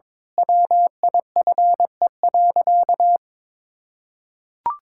Morse code, heard as a keyed tone, about 21 wpm, 700 Hz. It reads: WIFE.
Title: Pangram Challenge v2 - 22wpm